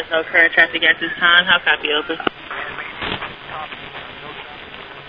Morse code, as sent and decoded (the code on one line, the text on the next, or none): none